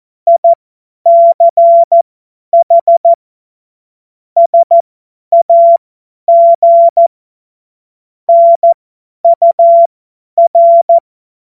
Morse code, heard as a keyed tone, about 14 wpm, spaced as for 10 wpm, 680 Hz. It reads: ICH SAG NUR